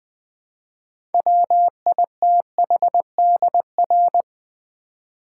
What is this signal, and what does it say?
Morse code, heard as a keyed tone, about 20 wpm, 700 Hz. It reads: WITHDR
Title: Build A Word - 3 Letters At A Time - 8-Letter Words - 20wpm